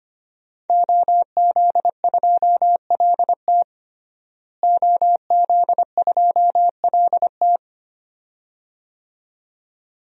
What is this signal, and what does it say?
Morse code, heard as a keyed tone, about 25 wpm, 700 Hz. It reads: OZ2LT OZ2LT
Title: DXCC-100 Call Signs - 5 Characters - Repeated 2x - 25wpm